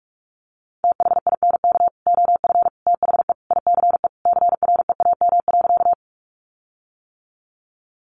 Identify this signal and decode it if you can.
Morse code, heard as a keyed tone, about 44 wpm, 700 Hz. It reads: THINK OF THE ICE CREAM.
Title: Sentences from Top 1500 Words 45wpm